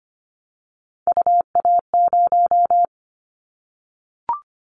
Morse code, heard as a keyed tone, about 25 wpm, 700 Hz. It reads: UA0
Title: Callsign Prefixes for Top 100 DXCC - 25wpm